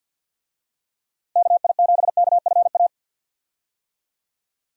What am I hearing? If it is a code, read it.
Morse code, heard as a keyed tone, about 47 wpm, 700 Hz. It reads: KI7KFA
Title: USA Call Signs - 6 Characters - Repeated 3x - 50wpm